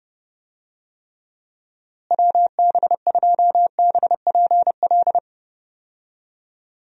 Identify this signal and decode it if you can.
Morse code, heard as a keyed tone, about 32 wpm, 700 Hz. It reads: WB2BPL
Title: USA Call Signs - 6 Characters - Repeated 3x - 30wpm